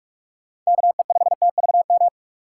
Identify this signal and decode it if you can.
Morse code, heard as a keyed tone, about 41 wpm, 700 Hz. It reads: KE5TVM
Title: USA Call Signs - 6 Characters - 3x Character Spacing - 45wpm